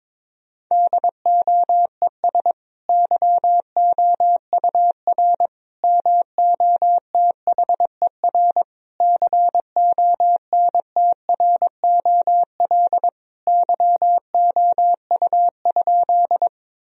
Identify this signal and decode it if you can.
Morse code, heard as a keyed tone, about 22 wpm, 700 Hz. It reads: DOES YOUR MOTHER CONTROL YOU?
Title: Sentences from Top 1000 Words - Review Part 3 - 22wpm